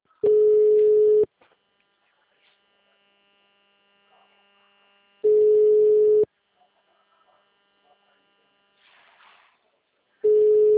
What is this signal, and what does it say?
Ringback tone (Europe, 425 Hz), with speech.